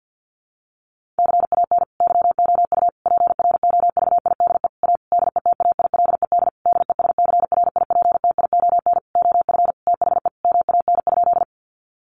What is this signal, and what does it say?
Morse code, heard as a keyed tone, about 50 wpm, 700 Hz. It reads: CAN YOU PROVIDE A DETAILED DESCRIPTION OF THE MAN?
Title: Sentences from Top 1500 Words 50wpm